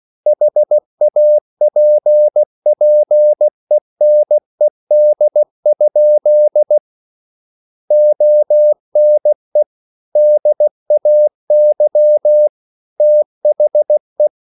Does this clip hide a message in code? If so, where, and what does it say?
Yes: Morse code from 0.3 to 14.3 s, heard as a keyed tone, about 16 wpm, 600 Hz. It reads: HAPPENED? ONE DAY THE